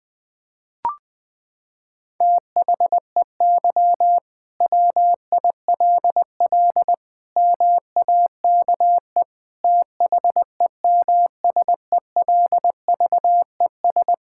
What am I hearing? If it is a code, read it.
Morse code, heard as a keyed tone, about 20 wpm, 700 Hz. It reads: THEY WILL MAKE THEMSELVES
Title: Sentences from Top 1600 Words 20wpm